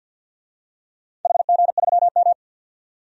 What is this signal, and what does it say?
Morse code, heard as a keyed tone, about 53 wpm, 700 Hz. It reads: HG3M
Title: DXCC-100 Call Signs - 4 Characters - 50wpm